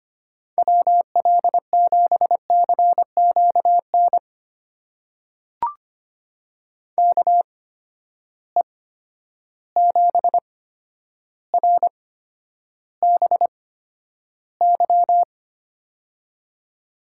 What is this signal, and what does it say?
Morse code, heard as a keyed tone, about 25 wpm, spaced as for 11 wpm, 700 Hz. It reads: WL7CQN KE7RBY
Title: USA Call Signs - 6 Characters - 8x Character Spacing - 25wpm